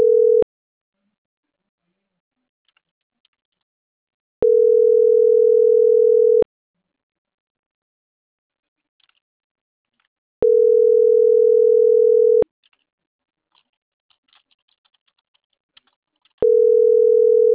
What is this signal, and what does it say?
Ringback tone (North America), 440 + 480 Hz.